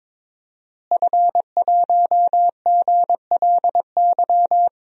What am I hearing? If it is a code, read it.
Morse code, heard as a keyed tone, about 22 wpm, 700 Hz. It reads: F1GLY
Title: DXCC-100 Call Signs - 5 Characters - Repeated 2x - 22wpm